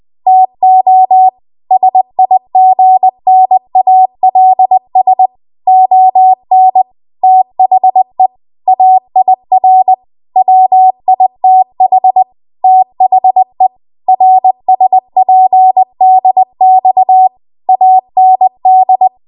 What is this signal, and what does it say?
Morse code, heard as a keyed tone, about 20 wpm, 750 Hz. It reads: TO SIGNALS ON THE AIR WITH THE RSPDX AND
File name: Archive / 20 WPM / 20WPM